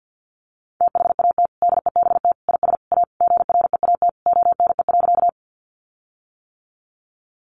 Morse code, heard as a keyed tone, about 50 wpm, 700 Hz. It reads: THAT DEBT IS A GREAT ONE.
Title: Sentences from Top 1300 Words 50wpm